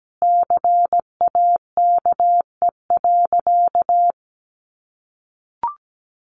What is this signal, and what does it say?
Morse code, heard as a keyed tone, about 17 wpm, 700 Hz. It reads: CAKE.
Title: Additional Easy Phrases 17wpm